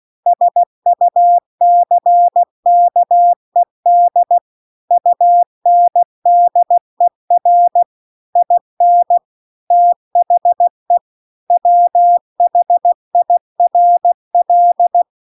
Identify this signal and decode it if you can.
Morse code, heard as a keyed tone, about 16 wpm, 700 Hz. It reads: SUCKED UNDER IN THE WHIRL
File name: CW-The-Big-Twoheaded-River-2-16WPM